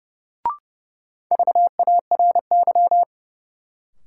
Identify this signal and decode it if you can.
Morse code, heard as a keyed tone, about 32 wpm, 700 Hz. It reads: VARY
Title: Sentences from Top 1200 Words - New Word Drill - 30wpm